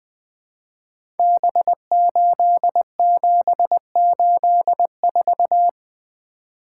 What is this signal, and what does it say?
Morse code, heard as a keyed tone, about 20 wpm, 700 Hz. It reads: B8784
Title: Code Groups - 5 Letter:Numbers - 20wpm